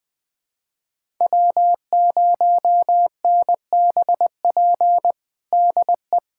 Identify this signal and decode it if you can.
Morse code, heard as a keyed tone, about 20 wpm, 700 Hz. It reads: W0NBP DE
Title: USA Callsign DE Callsign - 5 Characters - 20wpm